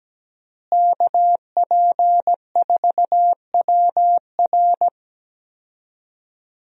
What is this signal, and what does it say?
Morse code, heard as a keyed tone, about 17 wpm, 700 Hz. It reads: KP4WR
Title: USA Call Signs - 5 Characters - Repeated 2x - 17wpm